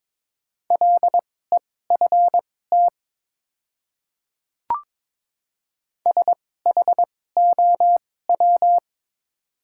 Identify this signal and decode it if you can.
Morse code, heard as a keyed tone, about 22 wpm, spaced as for 16 wpm, 700 Hz. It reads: LEFT SHOW
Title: 2x Character Spacing - Top 300 Words - 22wpm